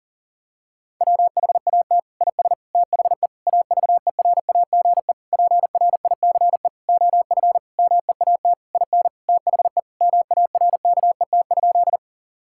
Morse code, heard as a keyed tone, about 37 wpm, 700 Hz. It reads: WHAT IS THE AVERAGE PRICE OF MEAT IN THE MARKET?
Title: Sentences from Top 1000 Words - Review Part 6 - 40wpm